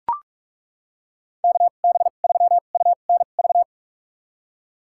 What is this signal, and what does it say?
Morse code, heard as a keyed tone, about 42 wpm, spaced as for 31 wpm, 700 Hz. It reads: KB3UNV
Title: USA Call Signs - 6 Characters - 2x Character Spacing - 45wpm